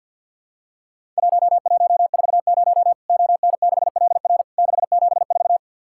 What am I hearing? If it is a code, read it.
Morse code, heard as a keyed tone, about 50 wpm, 700 Hz. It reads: 1140 ON6LR 674
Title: Worked All Europe DX Contest - QTC Exchange - 50wpm